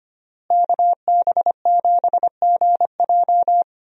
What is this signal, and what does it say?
Morse code, heard as a keyed tone, about 25 wpm, 700 Hz. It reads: KB7GJ
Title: USA Call Signs - 5 Characters - Repeated 3x - 25wpm